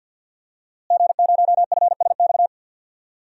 Morse code, heard as a keyed tone, about 46 wpm, 700 Hz. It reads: G0FSX
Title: DXCC-100 Call Signs - 5 Characters - Repeated 3x - 50wpm